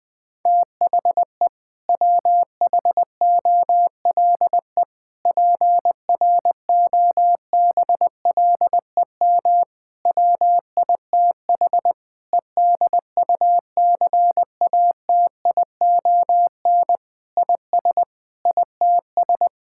Morse code, heard as a keyed tone, about 20 wpm, 700 Hz. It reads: THE WHOLE PROBLEM WITH EDUCATION IS ITS
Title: Sentences from Top 1000 Words - Review Part 6 - 20wpm